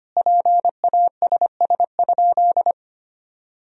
Morse code, heard as a keyed tone, about 25 wpm, 700 Hz. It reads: PASS?
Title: Sentences from Top 400 Words and Abbreviations 25wpm